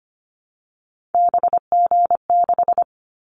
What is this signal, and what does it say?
Morse code, heard as a keyed tone, about 25 wpm, 700 Hz. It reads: BG6
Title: Callsign Prefixes for Top 100 DXCC - 25wpm